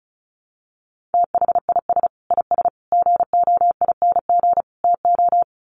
Morse code, heard as a keyed tone, about 34 wpm, 700 Hz. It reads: THIS IS GOING TO
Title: Sets of 4 Words 35wpm